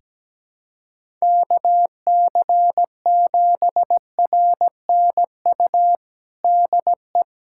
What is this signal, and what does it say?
Morse code, heard as a keyed tone, about 17 wpm, 700 Hz. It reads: KC7RNU DE
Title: USA Callsign DE Callsign - 6 Characters - 17wpm